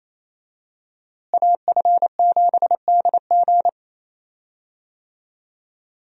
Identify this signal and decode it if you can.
Morse code, heard as a keyed tone, about 28 wpm, 700 Hz. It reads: AF7DG
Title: USA Call Signs - 5 Characters - Repeated 2x - 28wpm